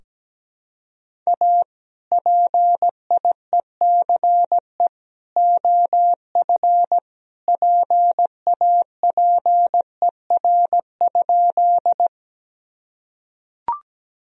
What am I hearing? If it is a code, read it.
Morse code, heard as a keyed tone, about 17 wpm, 700 Hz. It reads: A PIECE OF PAPER?
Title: Sentences from Top 1000 Words - Review Part 2 - 17wpm